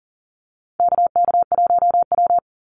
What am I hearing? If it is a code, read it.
Morse code, heard as a keyed tone, about 40 wpm, 700 Hz. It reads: KK1W
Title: NA SOTA Activator Callsigns 40wpm